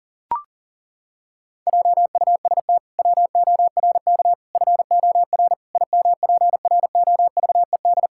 Morse code, heard as a keyed tone, about 37 wpm, 700 Hz. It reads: JUST WORK FOR IMPROVED
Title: Pangram Challenge v2 - 40wpm